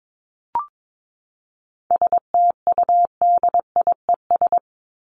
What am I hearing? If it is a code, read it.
Morse code, heard as a keyed tone, about 22 wpm, 700 Hz. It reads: STUDIES
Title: ES:S Suffix - Top 100 Words with Base Words 22wpm